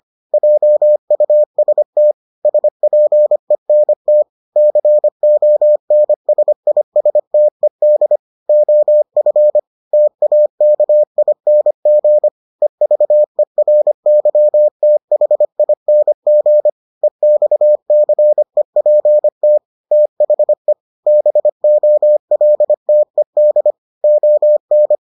Morse code, heard as a keyed tone, about 25 wpm, 600 Hz. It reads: JUST SPENT CONSISTED OF TAKING EVERYTHING EXCEPT THE BOLTED ON